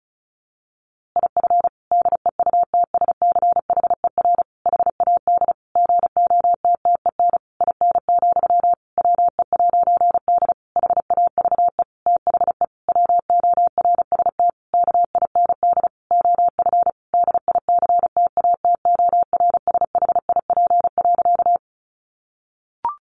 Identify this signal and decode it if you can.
Morse code, heard as a keyed tone, about 34 wpm, 700 Hz. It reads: IF DEUTSCHER HAD GOTTEN IN, WE'D HAVE THE WORST KIND OF DICTATORSHIP.
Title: A Sound Of Thunder 35wpm